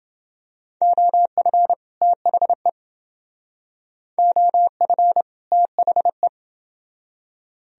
Morse code, heard as a keyed tone, about 30 wpm, 700 Hz. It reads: OF THE OF THE